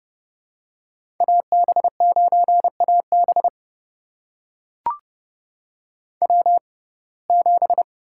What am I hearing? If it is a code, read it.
Morse code, heard as a keyed tone, about 30 wpm, spaced as for 22 wpm, 700 Hz. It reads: AB9AB W7
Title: USA Call Signs - 5 Characters - 6x Character Spacing - 30wpm